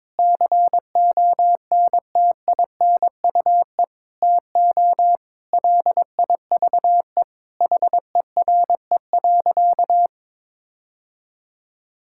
Morse code, heard as a keyed tone, about 22 wpm, 700 Hz. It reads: CONTINUE TO LIVE HERE.
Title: Sentences from Top 1000 Words - Review Part 6 - 22wpm